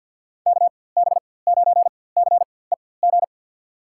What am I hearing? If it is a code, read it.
Morse code, heard as a keyed tone, about 51 wpm, spaced as for 23 wpm, 700 Hz. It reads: KB9CEG
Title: USA Call Signs - 6 Characters - 4x Character Spacing - 50wpm